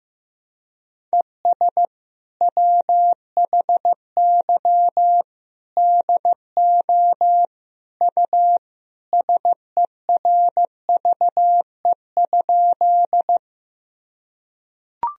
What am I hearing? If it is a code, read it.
Morse code, heard as a keyed tone, about 16 wpm, 700 Hz. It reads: ES WHY DO U SERVE?
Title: Sentences from Top 400 Words and Abbreviations 15wpm